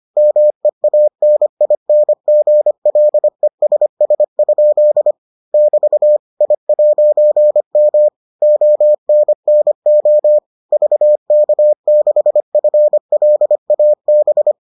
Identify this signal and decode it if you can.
Morse code, heard as a keyed tone, about 25 wpm, 600 Hz. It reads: MEANINGLESS? <BT> I'M ONNO VK6FLAB